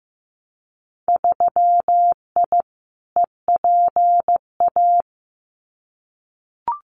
Morse code, heard as a keyed tone, about 15 wpm, 700 Hz. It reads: 3I EPA
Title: Winter Field Day Exchanges 15wpm